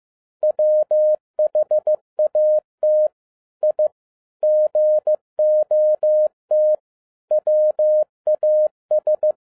Morse code, heard as a keyed tone, about 15 wpm, 610 Hz. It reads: WHAT I GOT WAS
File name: 2011-07-mp3 / n57 BACK FROM A BIG ACCIDENT 15wpm